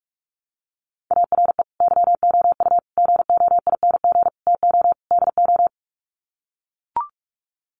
Morse code, heard as a keyed tone, about 44 wpm, 700 Hz. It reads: ARE YOU GOING TO DO
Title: Sets of 5 Words 45wpm